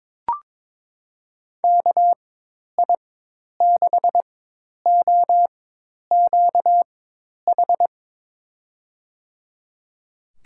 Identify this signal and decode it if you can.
Morse code, heard as a keyed tone, about 22 wpm, spaced as for 10 wpm, 700 Hz. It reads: KI6OQH
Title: USA Call Signs - 6 Characters - 4x Character Spacing - 22wpm